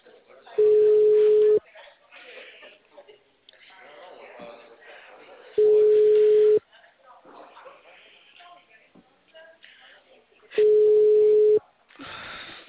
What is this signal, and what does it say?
Ringback tone (Europe, 425 Hz), with speech.
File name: out- -2015- - - .2049